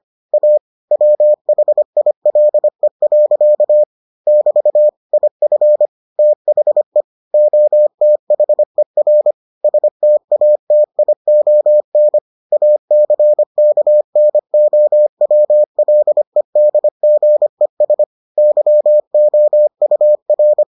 Morse code, heard as a keyed tone, about 25 wpm, 600 Hz. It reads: A WHILE. <BT> IF THE OTHER STATION ACKNOWLEDGES YOUR